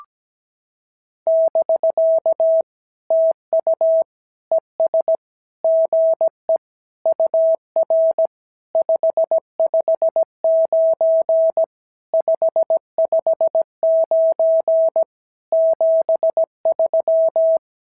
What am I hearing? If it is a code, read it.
Morse code, heard as a keyed tone, about 17 wpm, 650 Hz. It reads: <BK> TU ES GE UR 559 559 73
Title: SOTA QSO Practice 17wpm